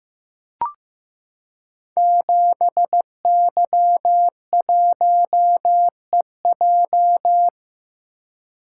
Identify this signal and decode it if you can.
Morse code, heard as a keyed tone, about 15 wpm, 700 Hz. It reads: 7Y1EJ
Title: Code Groups - 5 Letter:Numbers - 15wpm